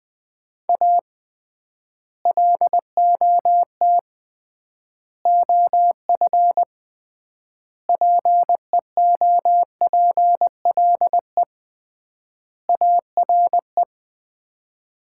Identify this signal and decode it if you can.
Morse code, heard as a keyed tone, about 20 wpm, 700 Hz. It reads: A LOT OF PEOPLE ARE
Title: 3x Word-Spacing - Sets of 5 Words - 20wpm